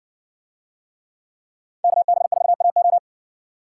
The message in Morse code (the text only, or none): KB4AO